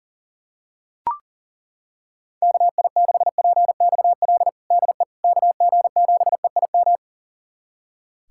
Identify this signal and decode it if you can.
Morse code, heard as a keyed tone, about 37 wpm, 700 Hz. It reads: KI6PXL DE KG7EIM
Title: USA Callsign DE Callsign 40wpm